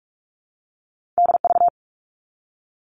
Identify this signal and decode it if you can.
Morse code, heard as a keyed tone, about 46 wpm, 700 Hz. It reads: DV
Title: All Cut Numbers - 2-Digit 45wpm